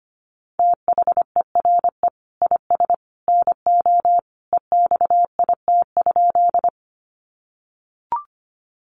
Morse code, heard as a keyed tone, about 25 wpm, 690 Hz. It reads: THERE IS NO EXIT?
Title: Sentences from Top 1100 Words 25wpm